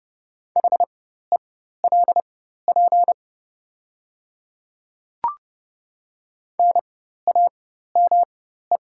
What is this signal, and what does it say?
Morse code, heard as a keyed tone, about 30 wpm, spaced as for 14 wpm, 700 Hz. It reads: HELP NAME
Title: 4x Character Spacing - Top 200 Words - 30wpm